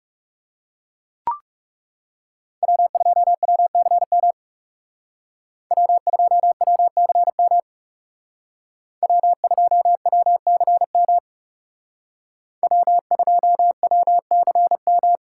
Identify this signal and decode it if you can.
Morse code, heard as a keyed tone, about 38 wpm, 700 Hz. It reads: W2WCM W2WCM W2WCM W2WCM